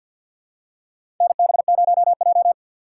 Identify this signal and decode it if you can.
Morse code, heard as a keyed tone, about 52 wpm, 700 Hz. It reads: NB0J
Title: USA Call Signs - 4 Characters - 6x Character Spacing - 50wpm